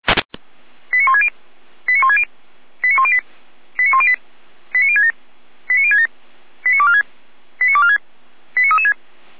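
Five-tone selective call (ZVEI), heard as quick runs of five tones, tones of about 70 ms. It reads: 89160 89160 89108 89108 89067 89067 89246 89246 89336
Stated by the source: SDR-received ZVEI five-tone call 89560.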